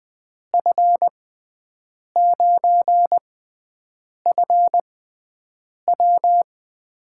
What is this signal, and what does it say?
Morse code, heard as a keyed tone, about 20 wpm, spaced as for 7 wpm, 700 Hz. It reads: F9FW
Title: USA Call Signs - 5 Characters - 6x Character Spacing - 20wpm